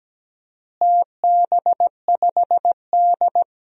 Morse code, heard as a keyed tone, about 17 wpm, 700 Hz. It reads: TB5D
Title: DXCC-100 Call Signs - 4 Characters - Repeated 3x - 17wpm